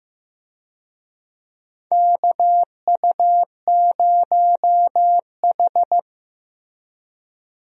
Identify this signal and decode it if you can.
Morse code, heard as a keyed tone, about 15 wpm, 700 Hz. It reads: KU0H
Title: USA Call Signs - 4 Characters - Repeated 3x - 15wpm